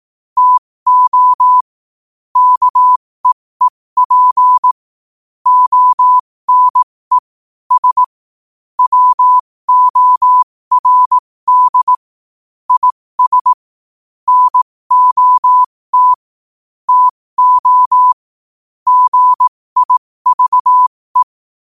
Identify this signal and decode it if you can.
Morse code, heard as a keyed tone, about 18 wpm, spaced as for 15 wpm, 1000 Hz. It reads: TO KEEP ONE S WORD IS NOT TO GIVE
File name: qotdfiles / 15wpm